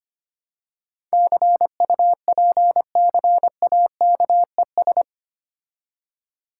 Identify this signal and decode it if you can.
Morse code, heard as a keyed tone, about 25 wpm, 700 Hz. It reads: CUPCAKES
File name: practice / 25wpm / Difficult Words with Hints - 25wpm